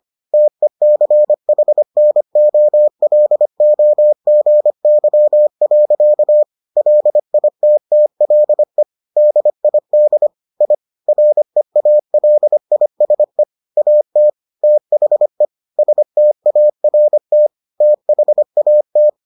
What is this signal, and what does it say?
Morse code, heard as a keyed tone, about 25 wpm, 600 Hz. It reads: TECHNOLOGY. LITTLE DID I REALISE AT THE START THAT